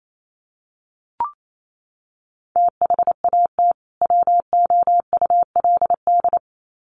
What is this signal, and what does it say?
Morse code, heard as a keyed tone, about 28 wpm, 700 Hz. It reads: THAT WOULD
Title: Sets of 2 Words 28wpm